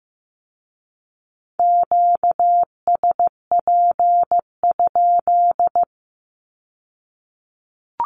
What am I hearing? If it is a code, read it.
Morse code, heard as a keyed tone, about 15 wpm, 700 Hz. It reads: QSP?
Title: Q-Codes with Spoken Meaning - 15wpm